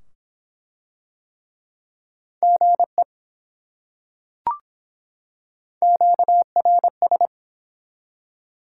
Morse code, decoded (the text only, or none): GE QRS